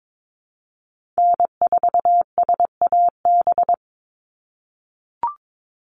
Morse code, heard as a keyed tone, about 22 wpm, 700 Hz. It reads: N4SAB
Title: USA Call Signs 22wpm